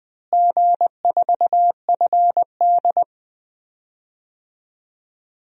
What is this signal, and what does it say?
Morse code, heard as a keyed tone, about 20 wpm, 700 Hz. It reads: G4FD
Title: DXCC-100 Call Signs - 4 Characters - Repeated 2x - 20wpm